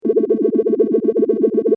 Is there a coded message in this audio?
No.